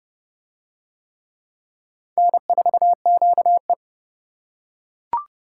Morse code, heard as a keyed tone, about 27 wpm, 690 Hz. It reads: N4QE